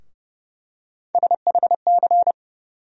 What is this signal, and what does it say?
Morse code, heard as a keyed tone, about 30 wpm, 700 Hz. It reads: SHC
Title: USA Call Sign Postfixes - 3 Letters 30wpm